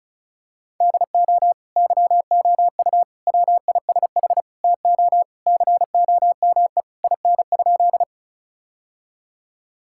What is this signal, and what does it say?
Morse code, heard as a keyed tone, about 36 wpm, 700 Hz. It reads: DO YOU WISH TO COME IN?
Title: Sentences from Top 1600 Words 35wpm